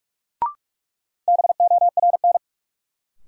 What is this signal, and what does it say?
Morse code, heard as a keyed tone, about 43 wpm, 700 Hz. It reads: BORN
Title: Sentences from Top 1000 Words - New Word Drill - 45wpm